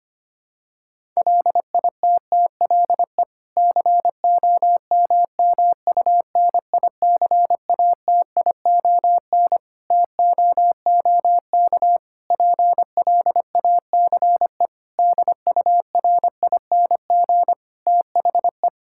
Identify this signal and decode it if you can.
Morse code, heard as a keyed tone, about 25 wpm, 700 Hz. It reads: LITTLE COMMUNICATION TOOK PLACE DURING THE
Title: Sentences from Top 1000 Words - Review Part 6 - 25wpm